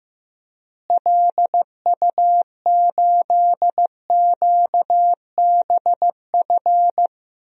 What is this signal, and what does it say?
Morse code, heard as a keyed tone, about 15 wpm, 700 Hz. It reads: LU8QBF
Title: DXCC-100 Call Signs - 6 Characters - Repeated 2x - 15wpm